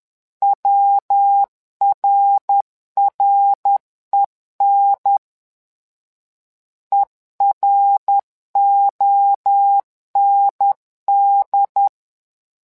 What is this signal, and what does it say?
Morse code, heard as a keyed tone, about 10 wpm, 800 Hz. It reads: WRREN EROND
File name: LEZIONI / 3 MWRO / 3 MWRO 10wpm 2148